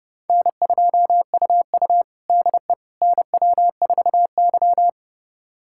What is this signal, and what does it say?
Morse code, heard as a keyed tone, about 30 wpm, 700 Hz. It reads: N2UU DE NW4Y
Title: USA Callsign DE Callsign - 4 Characters - 30wpm